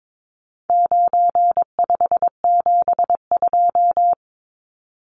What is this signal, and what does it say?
Morse code, heard as a keyed tone, about 22 wpm, 700 Hz. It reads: 9572
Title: Numbers - 4-Digit 22wpm